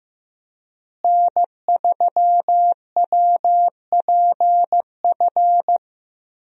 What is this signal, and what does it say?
Morse code, heard as a keyed tone, about 15 wpm, 700 Hz. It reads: N3WPF
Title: USA Call Signs - 5 Characters - Repeated 3x - 15wpm